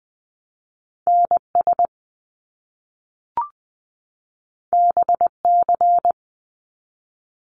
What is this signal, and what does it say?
Morse code, heard as a keyed tone, about 20 wpm, 700 Hz. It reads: NS BC
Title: Canadian Provinces and Territories 20wpm